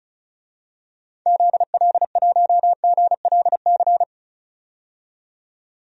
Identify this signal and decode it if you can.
Morse code, heard as a keyed tone, about 36 wpm, 700 Hz. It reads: ZL1GLC
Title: DXCC-100 Call Signs - 6 Characters - 35wpm